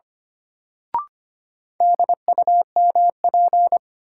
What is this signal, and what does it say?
Morse code, heard as a keyed tone, about 25 wpm, 700 Hz. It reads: DUMP
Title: Sentences from Top 1300 Words - New Word Drill Alt - 25wpm